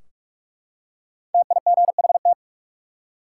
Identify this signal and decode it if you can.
Morse code, heard as a keyed tone, about 43 wpm, 700 Hz. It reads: TIGHT
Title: Five Letter Words - 45wpm